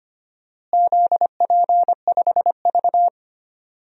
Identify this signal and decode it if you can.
Morse code, heard as a keyed tone, about 25 wpm, 700 Hz. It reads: ZP5V
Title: DXCC-100 Call Signs - 4 Characters - Repeated 3x - 25wpm